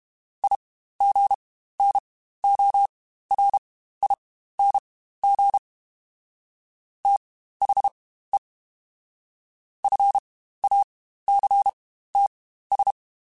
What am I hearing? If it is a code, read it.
Morse code, heard as a keyed tone, about 32 wpm, spaced as for 15 wpm, 800 Hz. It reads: IGNORING THE FACTS
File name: Quotes Spaced / Quotes Spaced-mp3s / IgnoringFacts sp @30